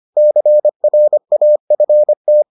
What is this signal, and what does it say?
Morse code, heard as a keyed tone, about 25 wpm, 600 Hz. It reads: CRAFT